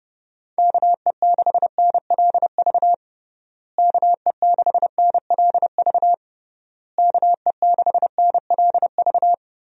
Morse code, heard as a keyed tone, about 30 wpm, 700 Hz. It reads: KE6NLV KE6NLV KE6NLV
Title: USA Call Signs - 6 Characters - Repeated 3x - 30wpm